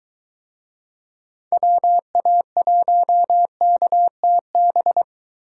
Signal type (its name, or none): Morse code